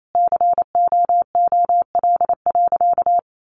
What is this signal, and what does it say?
Morse code, heard as a keyed tone, about 28 wpm, 700 Hz. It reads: COOL.
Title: Sentences from Top 1200 Words 28wpm